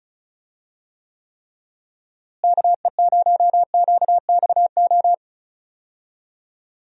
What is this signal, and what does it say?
Morse code, heard as a keyed tone, about 36 wpm, 700 Hz. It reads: KE0QXO
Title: USA Call Signs - 6 Characters - Repeated 3x - 35wpm